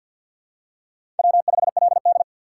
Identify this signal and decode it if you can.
Morse code, heard as a keyed tone, about 51 wpm, 700 Hz. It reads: W5LD
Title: USA Call Signs - 4 Characters - 3x Character Spacing - 50wpm